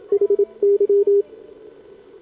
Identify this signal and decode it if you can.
Morse code, heard as a keyed tone, about 25 wpm, 410 Hz, with radio noise. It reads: HY
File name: CQWWCW2025 / 20M SCAN CN3A